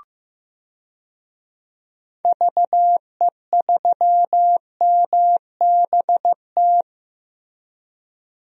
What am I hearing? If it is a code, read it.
Morse code, heard as a keyed tone, about 15 wpm, 700 Hz. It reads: VE3MBT
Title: DXCC-100 Call Signs - 6 Characters - Repeated 3x - 15wpm